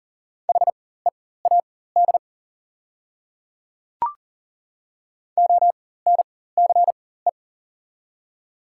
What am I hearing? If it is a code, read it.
Morse code, heard as a keyed tone, about 40 wpm, spaced as for 19 wpm, 700 Hz. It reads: HEAD ONCE
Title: 4x Character Spacing - Top 300 Words - 40wpm